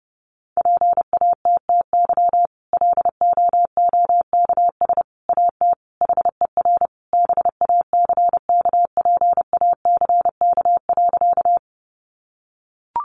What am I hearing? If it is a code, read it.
Morse code, heard as a keyed tone, about 30 wpm, 700 Hz. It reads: PATTY LOOKS AT HER BACKPACK.
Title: Additional Easy Phrases 30wpm